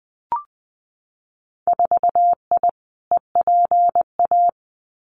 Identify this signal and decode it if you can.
Morse code, heard as a keyed tone, about 20 wpm, 700 Hz. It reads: 4I EPA
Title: Winter Field Day Exchanges Encore 20wpm